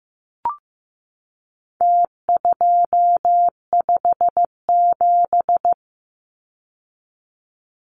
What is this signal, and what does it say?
Morse code, heard as a keyed tone, about 15 wpm, 700 Hz. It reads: T257
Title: Numbers - Contest Style - 4-Digit 15wpm